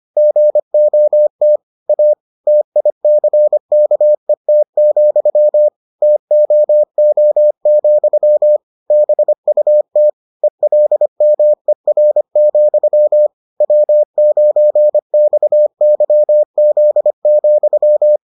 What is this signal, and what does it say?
Morse code, heard as a keyed tone, about 25 wpm, 600 Hz. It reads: GOT A TICKET, TOO, BUT ELMER, W9XYZ,